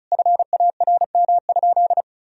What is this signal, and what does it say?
Morse code, heard as a keyed tone, about 34 wpm, 700 Hz. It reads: FARM?
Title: Sentences from Top 1000 Words - Review Part 2 - 35wpm